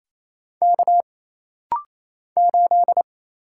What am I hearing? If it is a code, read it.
Morse code, heard as a keyed tone, about 27 wpm, 700 Hz. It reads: K 8
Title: Single Letter-Number - Rapid-Fire 28wpm